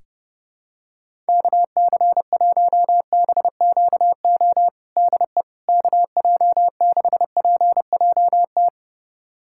Morse code, heard as a keyed tone, about 32 wpm, 700 Hz. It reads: KC1BQO DE KJ6PJT
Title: USA Callsign DE Callsign - 6 Characters - 30wpm